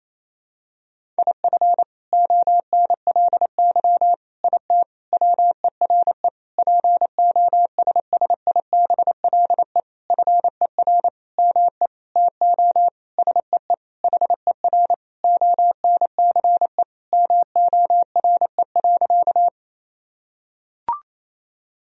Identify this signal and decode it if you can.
Morse code, heard as a keyed tone, about 28 wpm, 700 Hz. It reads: IF ONLY IT WERE POSSIBLE FER ME TO SEE HER ONCE MORE.
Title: Sentences from Top 500 Words and Abbreviations 28wpm